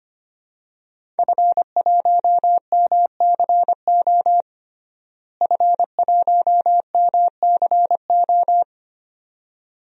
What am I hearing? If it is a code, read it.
Morse code, heard as a keyed tone, about 25 wpm, 700 Hz. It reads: F1MCO F1MCO
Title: DXCC-100 Call Signs - 5 Characters - Repeated 2x - 25wpm